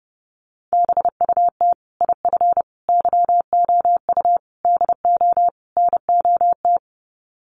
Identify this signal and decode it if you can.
Morse code, heard as a keyed tone, about 30 wpm, 700 Hz. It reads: BUT IF YOU DO NOT
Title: Sets of 5 Words Encore 30wpm